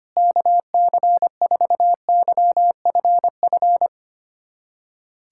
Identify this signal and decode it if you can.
Morse code, heard as a keyed tone, about 25 wpm, 700 Hz. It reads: KC4YFF
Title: USA Callsign DE Callsign - 6 Characters - 25wpm